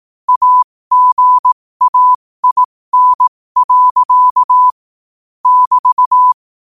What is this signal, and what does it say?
Morse code, heard as a keyed tone, about 18 wpm, spaced as for 15 wpm, 1000 Hz. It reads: AGAIN. <BT>
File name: qotdfiles / 15wpm